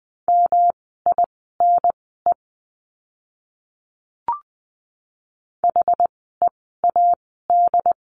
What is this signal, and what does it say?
Morse code, heard as a keyed tone, about 20 wpm, spaced as for 15 wpm, 700 Hz. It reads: MINE HEAD
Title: 108 Lesson 18f - Letter m - Words 20wpm